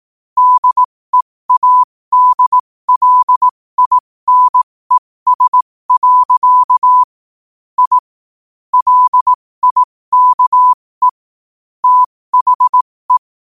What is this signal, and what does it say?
Morse code, heard as a keyed tone, about 18 wpm, spaced as for 15 wpm, 1000 Hz. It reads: DEADLINES. I LIKE THE
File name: qotdfiles / 15wpm